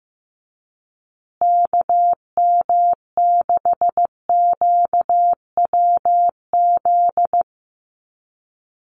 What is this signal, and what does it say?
Morse code, heard as a keyed tone, about 15 wpm, 700 Hz. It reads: KM6QWZ
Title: USA Call Signs Repeated 2x 15wpm